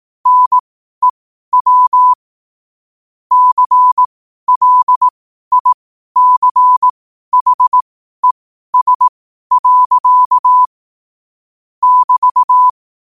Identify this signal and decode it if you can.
Morse code, heard as a keyed tone, about 18 wpm, spaced as for 12 wpm, 1000 Hz. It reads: NEW CLICHES. <BT>
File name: qotdfiles / 12wpm